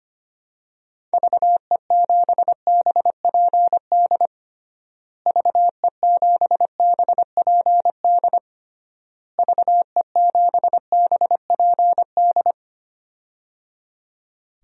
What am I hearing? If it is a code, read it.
Morse code, heard as a keyed tone, about 25 wpm, 700 Hz. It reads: VE7BPD VE7BPD VE7BPD
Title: DXCC-100 Call Signs - 6 Characters - Repeated 3x - 25wpm